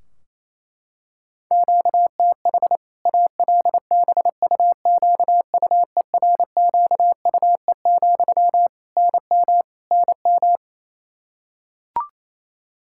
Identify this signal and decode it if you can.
Morse code, heard as a keyed tone, about 28 wpm, 700 Hz. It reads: QTH ALBUQUERQUE, NM NM
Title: K1USN SST QSO Practice 28wpm